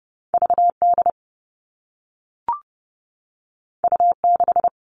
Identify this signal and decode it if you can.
Morse code, heard as a keyed tone, about 30 wpm, 700 Hz. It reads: VD U6
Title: All Cut Numbers - 2-Digit 30wpm